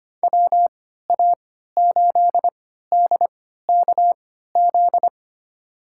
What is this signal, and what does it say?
Morse code, heard as a keyed tone, about 25 wpm, spaced as for 14 wpm, 700 Hz. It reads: WA8DKZ
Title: USA Call Signs - 6 Characters - 3x Character Spacing - 25wpm